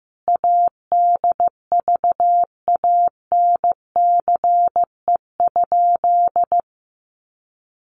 Morse code, heard as a keyed tone, about 15 wpm, 700 Hz. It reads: ADVANCE?
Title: Sentences from Top 1400 Words 15wpm